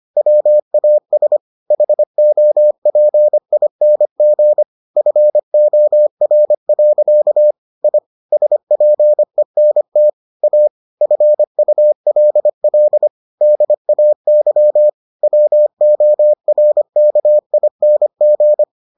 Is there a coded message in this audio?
Yes: Morse code from 0.2 to 18.6 s, heard as a keyed tone, about 25 wpm, 600 Hz. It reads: WAS HOPING FOR. I SPENT A FULL DAY WORKING